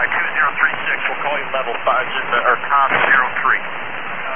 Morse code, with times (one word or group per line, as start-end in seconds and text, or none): none